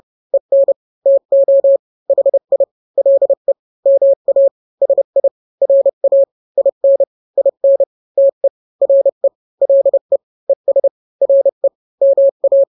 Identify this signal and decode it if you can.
Morse code, heard as a keyed tone, about 30 wpm, 550 Hz. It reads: EN TO HI LE MA SI RA IN IN TE RE LE ES RE MA
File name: n-grams / 2-grams 30 25 WPM